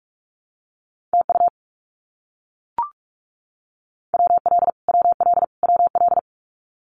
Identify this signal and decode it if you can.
Morse code, heard as a keyed tone, about 44 wpm, 700 Hz. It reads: TU WL WL WL
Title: Abbreviations Repeated 3x - 45wpm